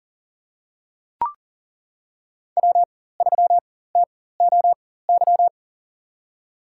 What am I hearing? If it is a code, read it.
Morse code, heard as a keyed tone, about 40 wpm, spaced as for 19 wpm, 700 Hz. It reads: W3TOY
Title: USA Call Signs - 4x Character Spacing - 40wpm